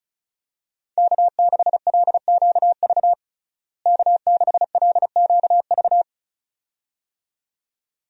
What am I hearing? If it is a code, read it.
Morse code, heard as a keyed tone, about 36 wpm, 700 Hz. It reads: K6LQV K6LQV
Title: USA Call Signs - 5 Characters - Repeated 2x - 35wpm